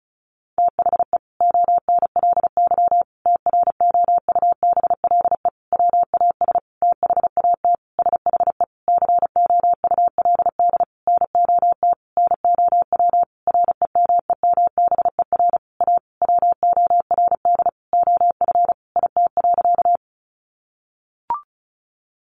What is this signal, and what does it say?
Morse code, heard as a keyed tone, about 34 wpm, 700 Hz. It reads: THE ONLY TROUBLE WAS THAT SHE COULD NOT NOW REMEMBER A WORD OF IT.
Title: Sentences from Top 1200 Words 35wpm